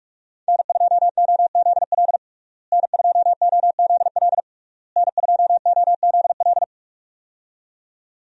Morse code, heard as a keyed tone, about 46 wpm, 700 Hz. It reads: N2OZL N2OZL N2OZL
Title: USA Call Signs - 5 Characters - Repeated 3x - 45wpm